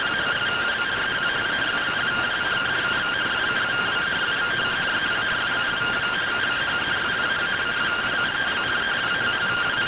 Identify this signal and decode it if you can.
Two-tone teleprinter signal, about 170 Hz shift.
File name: findings / CT AXIAL -111.gen-41 rtty rx